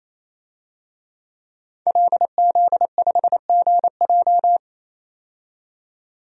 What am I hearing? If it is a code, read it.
Morse code, heard as a keyed tone, about 28 wpm, 700 Hz. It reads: LZ5GJ
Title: DXCC-100 Call Signs - 5 Characters - 28wpm